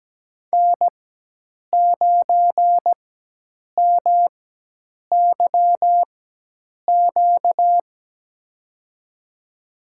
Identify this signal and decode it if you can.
Morse code, heard as a keyed tone, about 17 wpm, spaced as for 8 wpm, 700 Hz. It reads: N9MYQ
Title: USA Call Signs - 4x Character Spacing - 17wpm